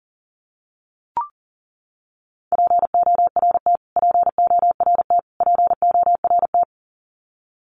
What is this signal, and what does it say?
Morse code, heard as a keyed tone, about 40 wpm, 700 Hz. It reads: PORT PORT PORT
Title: Top 200 Words Repeated 40wpm